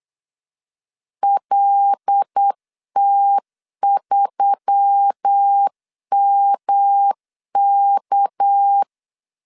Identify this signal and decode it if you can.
Morse code, heard as a keyed tone, about 8 wpm, 800 Hz. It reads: LT3MK
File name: CW / DET / 1 6 1500C S C 8wpm 5219